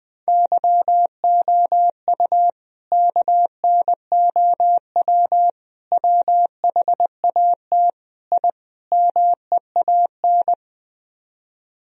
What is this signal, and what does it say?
Morse code, heard as a keyed tone, about 20 wpm, 700 Hz. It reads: YOU KNOW WHAT I MEAN
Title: Speed-Racing - Sets of 5 Words 20wpm